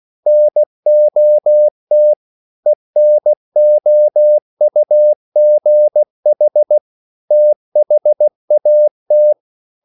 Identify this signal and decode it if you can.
Morse code, heard as a keyed tone, about 16 wpm, 600 Hz. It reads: NOT ENOUGH THAT
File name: CW / Practice / EnglishQuotations0416